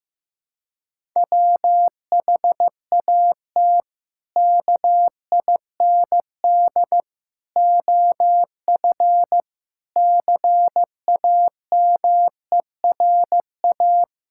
Morse code, heard as a keyed tone, about 15 wpm, 700 Hz. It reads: WHAT KIND OF CAMERA
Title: Sentences from Top 1000 Words - Review Part 5 - 15wpm